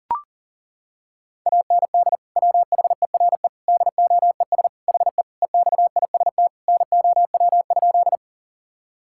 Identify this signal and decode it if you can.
Morse code, heard as a keyed tone, about 37 wpm, 700 Hz. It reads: AND WHERE DOES HE EXIST NOW?
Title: Sentences from Top 1000 Words - Review Part 5 - 40wpm